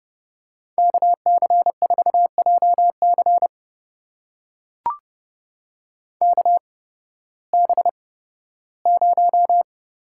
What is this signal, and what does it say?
Morse code, heard as a keyed tone, about 30 wpm, spaced as for 16 wpm, 700 Hz. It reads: KC4JC KB0
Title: USA Call Signs - 5 Characters - 8x Character Spacing - 30wpm